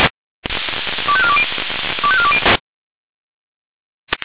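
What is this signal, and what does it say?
Five-tone selective call (ZVEI), heard as quick runs of five tones, tones of about 70 ms. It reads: 25420 25420
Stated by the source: ZVEI five-tone selective call received with SDR.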